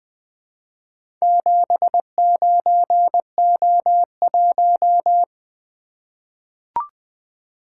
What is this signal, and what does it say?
Morse code, heard as a keyed tone, about 20 wpm, 700 Hz. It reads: 79O1
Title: Four Letter-Number Combinations - 20wpm